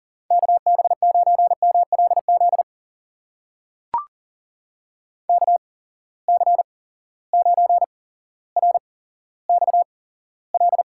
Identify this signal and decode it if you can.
Morse code, heard as a keyed tone, about 40 wpm, spaced as for 17 wpm, 700 Hz. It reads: KB9MLZ KC9RXL
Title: USA Call Signs - 8x Character Spacing - 40wpm